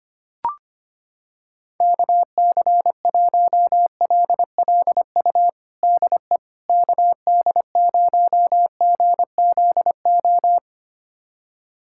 Morse code, heard as a keyed tone, about 25 wpm, 700 Hz. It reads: KC1LLU DE KD0GZO
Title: USA Callsign DE Callsign 25wpm